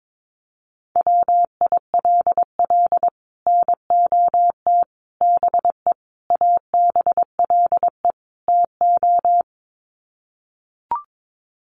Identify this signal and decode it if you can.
Morse code, heard as a keyed tone, about 22 wpm, 700 Hz. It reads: WILL NOT BE ABLE TO
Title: Sets of 5 Words 22wpm